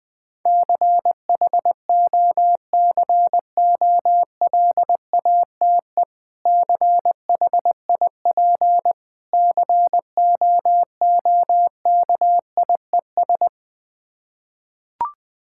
Morse code, heard as a keyed tone, about 21 wpm, 700 Hz. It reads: CHOCOLATE CHIP COOKIES
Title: Difficult Words with Hints - 20wpm